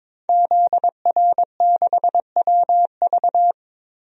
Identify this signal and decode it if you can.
Morse code, heard as a keyed tone, about 22 wpm, 700 Hz. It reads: ZR6WV
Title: DXCC-100 Call Signs - 5 Characters - 22wpm